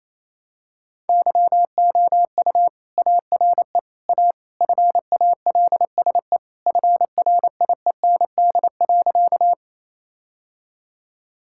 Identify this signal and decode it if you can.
Morse code, heard as a keyed tone, about 27 wpm, 700 Hz. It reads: YOU ARE A FALSE FRIEND.
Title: Sentences from Top 1600 Words 28wpm